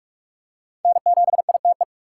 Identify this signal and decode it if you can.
Morse code, heard as a keyed tone, about 44 wpm, 700 Hz. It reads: N7ITE
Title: USA Call Signs - 5 Characters - 6x Character Spacing - 45wpm